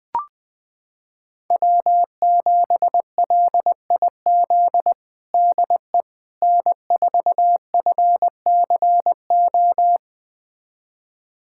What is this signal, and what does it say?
Morse code, heard as a keyed tone, about 20 wpm, 700 Hz. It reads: W7LIZ DE N4FCO
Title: USA Callsign DE Callsign - 5 Characters - 20wpm